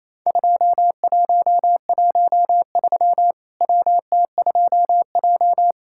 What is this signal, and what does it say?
Morse code, heard as a keyed tone, about 28 wpm, 700 Hz. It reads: 2113 WT2J
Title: Worked All Europe DX Contest - QTC Exchange - 28wpm